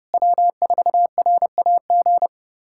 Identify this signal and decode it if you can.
Morse code, heard as a keyed tone, about 30 wpm, 700 Hz. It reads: W4RAG
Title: USA Callsign DE Callsign - 5 Characters - 30wpm